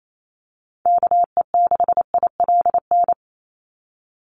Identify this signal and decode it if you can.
Morse code, heard as a keyed tone, about 28 wpm, 700 Hz. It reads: KE6ILN
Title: USA Call Signs 28wpm